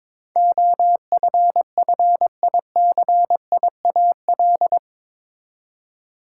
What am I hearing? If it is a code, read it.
Morse code, heard as a keyed tone, about 22 wpm, 700 Hz. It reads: OFFICIAL
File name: practice / 22wpm / Eight Letter Words - 22wpm